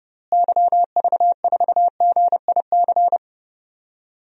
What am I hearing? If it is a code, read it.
Morse code, heard as a keyed tone, about 30 wpm, 700 Hz. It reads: YV4GIC
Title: DXCC-100 Call Signs - 6 Characters - Repeated 3x - 30wpm